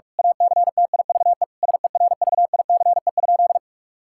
Morse code, heard as a keyed tone, about 46 wpm, 700 Hz. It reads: ACTIVE SERVICE?
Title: Sentences from Top 1000 Words - Review Part 6 - 45wpm